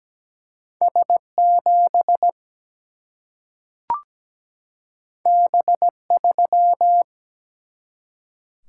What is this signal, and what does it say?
Morse code, heard as a keyed tone, about 17 wpm, 700 Hz. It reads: S7 B3
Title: Single-Letter Single-Number 17wpm